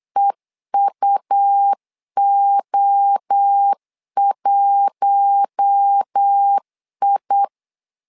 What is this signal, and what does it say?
Morse code, heard as a keyed tone, about 8 wpm, 800 Hz. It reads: EUO1I
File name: LEZIONI / 4 Q1 8wpm 3035